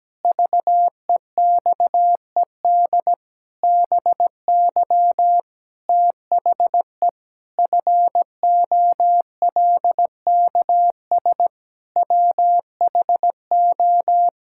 Morse code, heard as a keyed tone, about 17 wpm, 700 Hz. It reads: VEXED BY THE FOLKS WHO
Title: Pangram Challenge v2 - 17wpm